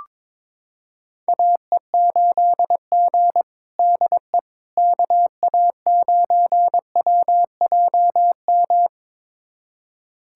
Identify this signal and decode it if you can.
Morse code, heard as a keyed tone, about 22 wpm, 700 Hz. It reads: AE8G DE KA9WJM
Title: USA Callsign DE Callsign 22wpm